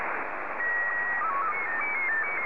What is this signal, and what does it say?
SSTV (slow-scan television), Scottie 1.